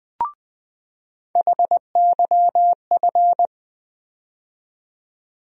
Morse code, heard as a keyed tone, about 20 wpm, 700 Hz. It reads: HYF
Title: USA Call Sign Postfixes - 3 Letters 20wpm